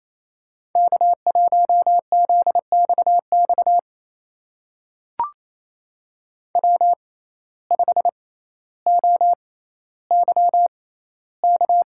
Morse code, heard as a keyed tone, about 28 wpm, spaced as for 14 wpm, 700 Hz. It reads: K1ZXX W5OYK
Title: USA Call Signs - 6x Character Spacing - 28wpm